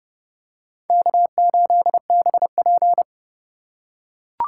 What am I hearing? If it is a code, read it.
Morse code, heard as a keyed tone, about 32 wpm, 700 Hz. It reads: K8BP
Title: USA Call Signs - 4 Characters - 4x Character Spacing - 30wpm